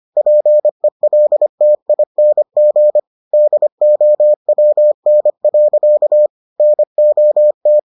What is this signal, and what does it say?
Morse code, heard as a keyed tone, about 25 wpm, 600 Hz. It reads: PELTING DOWN. NOT